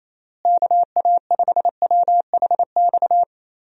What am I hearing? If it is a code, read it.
Morse code, heard as a keyed tone, about 28 wpm, 700 Hz. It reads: KA5WHX
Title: USA Call Signs - 6 Characters - Repeated 3x - 28wpm